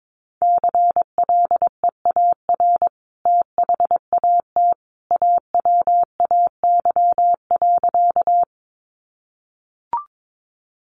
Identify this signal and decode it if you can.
Morse code, heard as a keyed tone, about 22 wpm, 700 Hz. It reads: CLEAR THAT AWAY.
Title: Sentences from Top 600 Words 22wpm